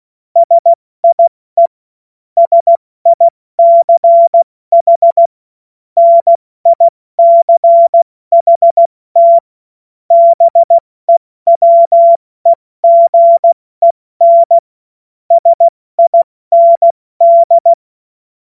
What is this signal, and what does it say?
Morse code, heard as a keyed tone, about 16 wpm, 680 Hz. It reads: SIE SICH NICHT BEWEGEN SIND
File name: Morsekurs / Witze / Witz 06. Januar 2026 70 F80